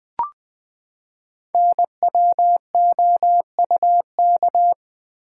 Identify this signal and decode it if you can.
Morse code, heard as a keyed tone, about 20 wpm, 700 Hz. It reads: NWOUK
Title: Code Groups - 5 Letters - 20wpm